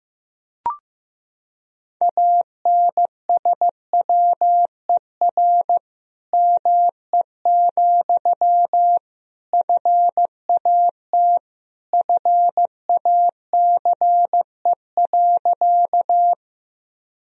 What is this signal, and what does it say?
Morse code, heard as a keyed tone, about 15 wpm, 700 Hz. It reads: ANSWER ME, FAT FACE.
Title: Sentences from Top 1000 Words - Review Part 4 - 15wpm